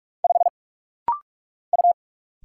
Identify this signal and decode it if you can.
Morse code, heard as a keyed tone, about 46 wpm, 700 Hz. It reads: 5 U
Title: Single Letter-Number Alt - Rapid-Fire 45wpm